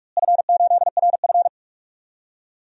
Morse code, heard as a keyed tone, about 47 wpm, 700 Hz. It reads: F8RF
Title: DXCC-100 Call Signs - 4 Characters - Repeated 3x - 45wpm